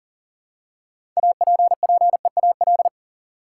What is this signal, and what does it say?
Morse code, heard as a keyed tone, about 40 wpm, 700 Hz. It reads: APPEAL
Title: Six Letter Words - 40wpm